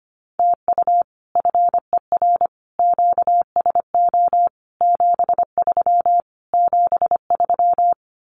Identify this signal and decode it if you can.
Morse code, heard as a keyed tone, about 25 wpm, 700 Hz. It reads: TU FER QSO 73 73
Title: Common QSO Phrases Encore 25wpm